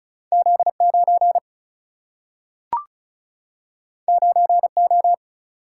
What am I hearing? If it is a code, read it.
Morse code, heard as a keyed tone, about 36 wpm, 700 Hz. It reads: Z9 9O
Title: Single-Letter Single-Number 35wpm